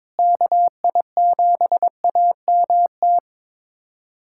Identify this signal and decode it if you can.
Morse code, heard as a keyed tone, about 21 wpm, 700 Hz. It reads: KI7AMT
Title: USA Callsign DE Callsign - 6 Characters - 22wpm